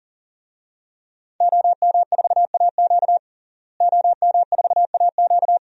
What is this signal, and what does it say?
Morse code, heard as a keyed tone, about 40 wpm, 700 Hz. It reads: OM4AQ OM4AQ
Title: DXCC-100 Call Signs - 5 Characters - Repeated 3x - 40wpm